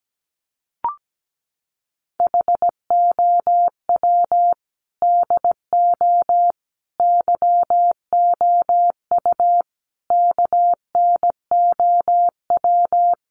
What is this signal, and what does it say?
Morse code, heard as a keyed tone, about 17 wpm, 700 Hz. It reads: HOW DO YOU KNOW
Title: Sets of 4 Words 17wpm